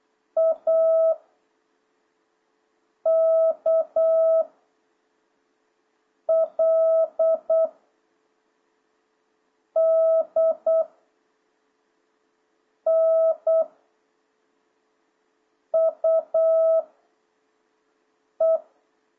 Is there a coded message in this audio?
Yes: Morse code from 0.4 to 18.6 s, heard as a keyed tone, about 8 wpm, 650 Hz. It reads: A K L D N U E